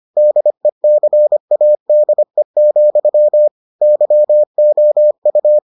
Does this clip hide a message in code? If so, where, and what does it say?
Yes: Morse code from 0.2 to 5.6 s, heard as a keyed tone, about 25 wpm, 600 Hz. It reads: DECADE, YOU